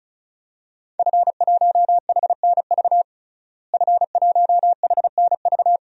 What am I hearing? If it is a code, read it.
Morse code, heard as a keyed tone, about 36 wpm, 700 Hz. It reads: F1HNV F1HNV
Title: DXCC-100 Call Signs - 5 Characters - Repeated 2x - 35wpm